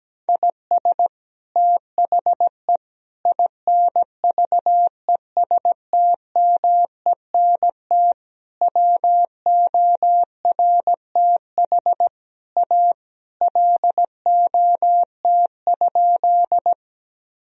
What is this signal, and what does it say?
Morse code, heard as a keyed tone, about 17 wpm, 700 Hz. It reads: IS THE INVESTMENT WORTH A LOT?
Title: Sentences from Top 1000 Words - Review Part 4 - 17wpm